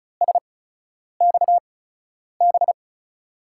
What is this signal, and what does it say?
Morse code, heard as a keyed tone, about 36 wpm, spaced as for 10 wpm, 700 Hz. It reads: SXB
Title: USA Call Signs - 6 Characters - 8x Character Spacing - 35wpm